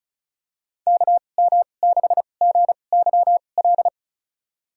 Morse code, heard as a keyed tone, about 36 wpm, spaced as for 25 wpm, 700 Hz. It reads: KM6GYL
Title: USA Call Signs - 2x Character Spacing - 35wpm